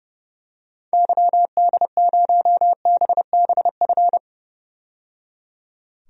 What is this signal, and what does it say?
Morse code, heard as a keyed tone, about 30 wpm, 700 Hz. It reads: YD0BBF
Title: DXCC-100 Call Signs - 6 Characters - 30wpm